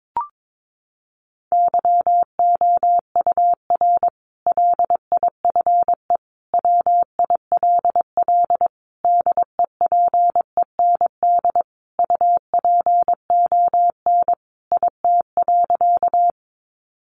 Morse code, heard as a keyed tone, about 22 wpm, 700 Hz. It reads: YOUR LIFE WILL DEPEND UPON IT.
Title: Sentences from Top 1300 Words 22wpm